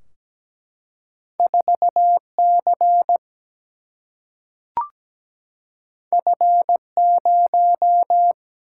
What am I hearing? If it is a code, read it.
Morse code, heard as a keyed tone, about 17 wpm, 700 Hz. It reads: 4C F0
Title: Single-Letter Single-Number 17wpm